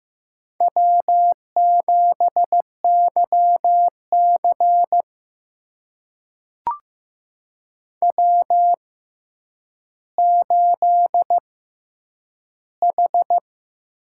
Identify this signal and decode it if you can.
Morse code, heard as a keyed tone, about 15 wpm, spaced as for 9 wpm, 700 Hz. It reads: W7YC W8H
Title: USA Call Signs - 4 Characters - 6x Character Spacing - 15wpm